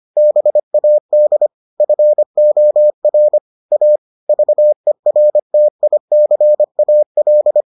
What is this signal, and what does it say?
Morse code, heard as a keyed tone, about 25 wpm, 600 Hz. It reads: BAD FOR A VERTICAL